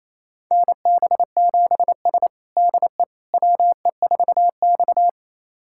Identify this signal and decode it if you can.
Morse code, heard as a keyed tone, about 28 wpm, 700 Hz. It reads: NB7S DE WE4X
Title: USA Callsign DE Callsign - 4 Characters - 28wpm